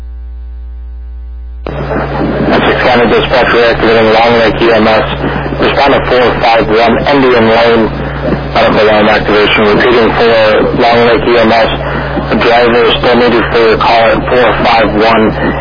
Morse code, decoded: none

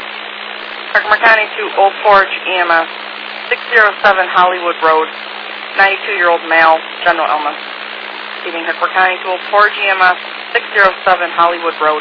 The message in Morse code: none

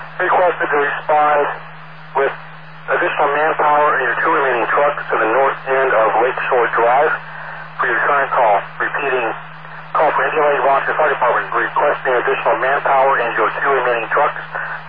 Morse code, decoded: none

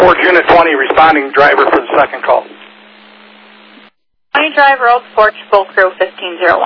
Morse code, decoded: none

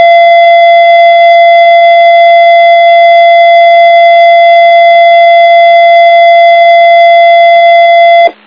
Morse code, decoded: none

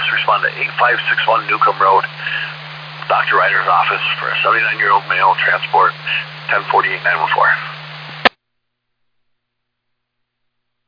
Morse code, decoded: none

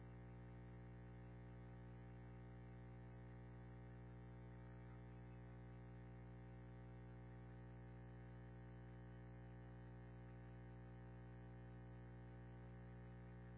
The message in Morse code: none

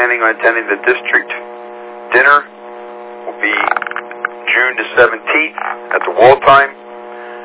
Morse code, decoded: none

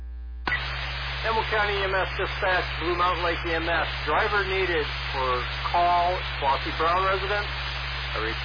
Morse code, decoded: none

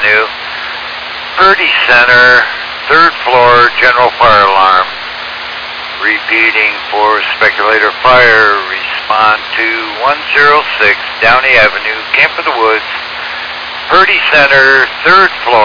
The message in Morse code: none

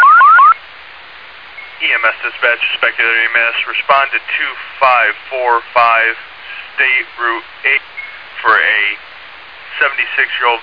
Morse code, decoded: none

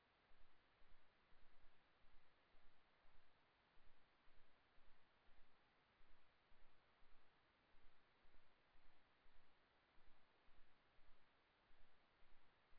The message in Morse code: none

